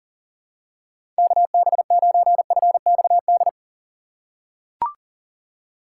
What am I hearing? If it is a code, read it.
Morse code, heard as a keyed tone, about 40 wpm, 700 Hz. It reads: KB9FXD
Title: USA Call Signs - 6 Characters - 4x Character Spacing - 40wpm